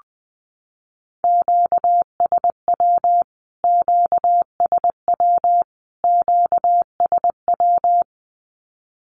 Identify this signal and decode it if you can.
Morse code, heard as a keyed tone, about 20 wpm, 700 Hz. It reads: QSW QSW QSW
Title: Q-Codes Repeated 3x 20wpm